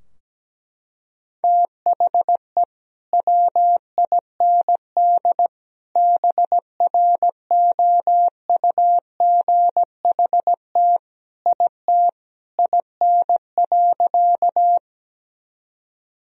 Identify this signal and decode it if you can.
Morse code, heard as a keyed tone, about 17 wpm, 700 Hz. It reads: THE WIND BROUGHT IT IN.
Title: Sentences from Top 1000 Words - Review Part 2 - 17wpm